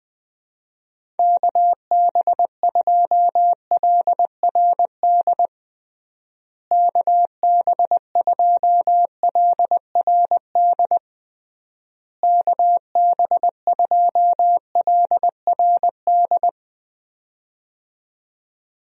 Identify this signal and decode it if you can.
Morse code, heard as a keyed tone, about 20 wpm, 700 Hz. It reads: KB2LRD KB2LRD KB2LRD
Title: USA Call Signs - 6 Characters - Repeated 3x - 20wpm